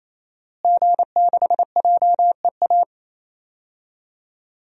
Morse code, heard as a keyed tone, about 28 wpm, 700 Hz. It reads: G6JEA
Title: DXCC-100 Call Signs - 5 Characters - Repeated 3x - 28wpm